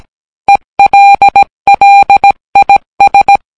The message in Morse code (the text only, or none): ELLIS